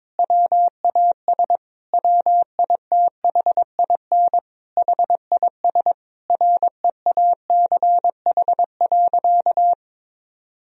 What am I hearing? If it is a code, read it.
Morse code, heard as a keyed tone, about 22 wpm, 700 Hz. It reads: WAS WITHIN HIS REACH.
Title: Sentences from Top 1000 Words - Review Part 4 - 22wpm